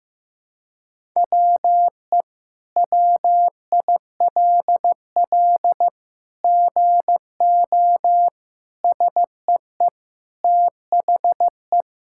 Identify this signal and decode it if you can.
Morse code, heard as a keyed tone, about 15 wpm, 700 Hz. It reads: WE WILL GO SEE THE
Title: Sentences from Top 1000 Words - Review Part 6 - 15wpm